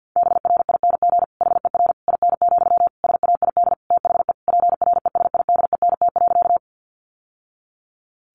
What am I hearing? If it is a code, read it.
Morse code, heard as a keyed tone, about 49 wpm, 700 Hz. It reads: BRING HER IN, SAID THE PRESIDENT.
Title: Sentences from Top 1000 Words 50wpm